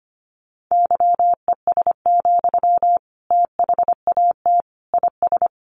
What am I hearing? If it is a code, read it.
Morse code, heard as a keyed tone, about 25 wpm, 700 Hz. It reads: YES, THAT IS
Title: Sentences from Top 500 Words 25wpm